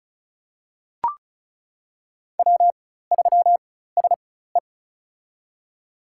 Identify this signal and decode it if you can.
Morse code, heard as a keyed tone, about 36 wpm, spaced as for 16 wpm, 700 Hz. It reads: W3SE
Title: USA Call Signs - 4x Character Spacing - 35wpm